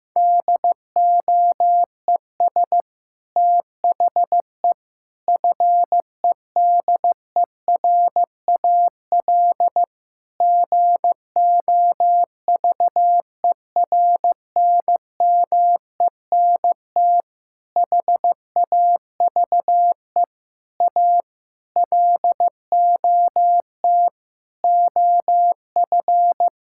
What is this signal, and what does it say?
Morse code, heard as a keyed tone, about 15 wpm, 700 Hz. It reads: DOES THE FEDERAL GOVERNMENT HAVE A LOT OF
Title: Sentences from Top 1000 Words - Review Part 5 - 15wpm